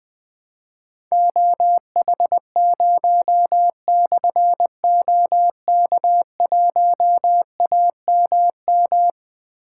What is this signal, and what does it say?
Morse code, heard as a keyed tone, about 20 wpm, 700 Hz. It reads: OH0/OK1AMM
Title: Difficult Call Signs 20wpm